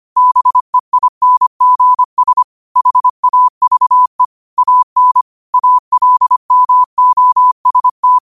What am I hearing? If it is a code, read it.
Morse code, heard as a keyed tone, about 25 wpm, 1000 Hz. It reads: BEINGS HAVE AN ALMOST
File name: qotdfiles / 25wpm